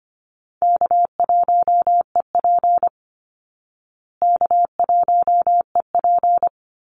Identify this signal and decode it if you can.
Morse code, heard as a keyed tone, about 25 wpm, 700 Hz. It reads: K1EP K1EP
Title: Easy Call Signs Repeated 2x - 25wpm